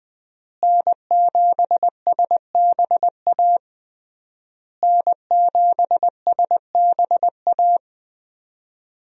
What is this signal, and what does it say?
Morse code, heard as a keyed tone, about 20 wpm, 700 Hz. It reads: N7SBA N7SBA
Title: USA Call Signs - 5 Characters - Repeated 3x - 20wpm